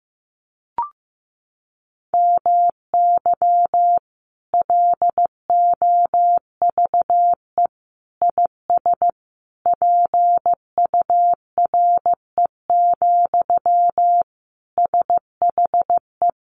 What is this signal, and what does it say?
Morse code, heard as a keyed tone, about 15 wpm, 700 Hz. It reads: MY LOVE IS PURE, SHE
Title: Sentences from Top 1500 Words 15wpm